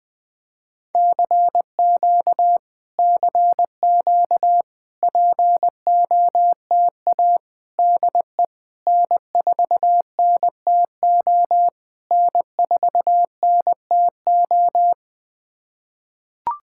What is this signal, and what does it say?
Morse code, heard as a keyed tone, about 20 wpm, 700 Hz. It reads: CQ CQ POTA DE N4NTO N4NTO
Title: POTA QSO Practice - 20wpm